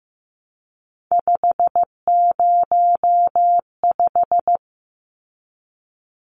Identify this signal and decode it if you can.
Morse code, heard as a keyed tone, about 15 wpm, 700 Hz. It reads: 505
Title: Numbers - 3-Digit 15wpm